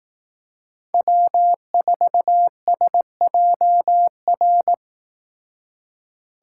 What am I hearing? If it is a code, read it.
Morse code, heard as a keyed tone, about 18 wpm, 700 Hz. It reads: W4SJR
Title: Speed Racing - USA Call Signs 15wpm